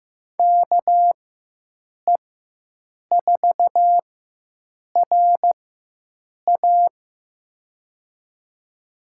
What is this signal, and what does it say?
Morse code, heard as a keyed tone, about 15 wpm, spaced as for 7 wpm, 700 Hz. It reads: KE4RA
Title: USA Call Signs - 4x Character Spacing - 15wpm